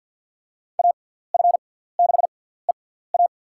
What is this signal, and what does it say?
Morse code, heard as a keyed tone, about 50 wpm, spaced as for 17 wpm, 700 Hz. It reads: AF6EA
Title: USA Call Signs - 5 Characters - 6x Character Spacing - 50wpm